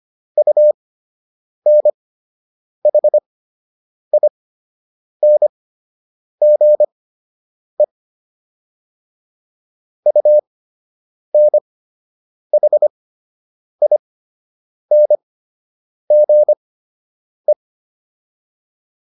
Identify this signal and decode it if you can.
Morse code, heard as a keyed tone, about 25 wpm, spaced as for 8 wpm, 600 Hz. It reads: UNHINGE UNHINGE